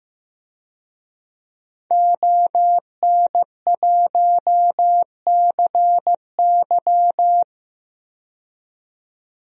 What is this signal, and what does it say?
Morse code, heard as a keyed tone, about 15 wpm, 700 Hz. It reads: ON1CY
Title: DXCC-100 Call Signs - 5 Characters - 15wpm